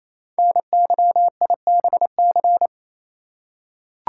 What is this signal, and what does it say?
Morse code, heard as a keyed tone, about 28 wpm, 700 Hz. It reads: NYIBC
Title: Code Groups - 5 Letter:Numbers - 28wpm